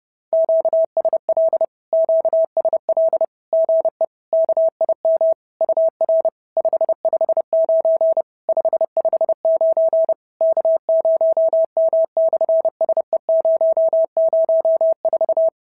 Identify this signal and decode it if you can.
Morse code, heard as a keyed tone, about 30 wpm, 650 Hz. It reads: QSL QSL GE KIM UR 559 559 K0M/SE004
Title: SOTA QSO Practice 30wpm